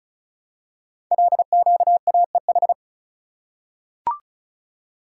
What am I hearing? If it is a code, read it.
Morse code, heard as a keyed tone, about 36 wpm, 700 Hz. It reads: LQAEH
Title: Code Groups - 5 Letters - 35wpm